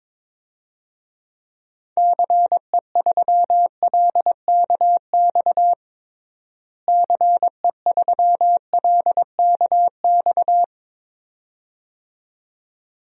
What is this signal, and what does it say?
Morse code, heard as a keyed tone, about 22 wpm, 700 Hz. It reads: CE3LKX CE3LKX
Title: DXCC-100 Call Signs - 6 Characters - Repeated 2x - 22wpm